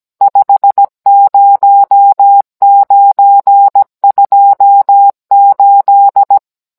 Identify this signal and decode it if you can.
Morse code, heard as a keyed tone, about 17 wpm, 800 Hz. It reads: 50928